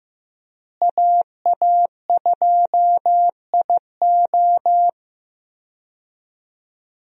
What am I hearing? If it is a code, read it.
Morse code, heard as a keyed tone, about 15 wpm, 700 Hz. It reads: AA2IO
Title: USA Call Signs - 5 Characters - Repeated 3x - 15wpm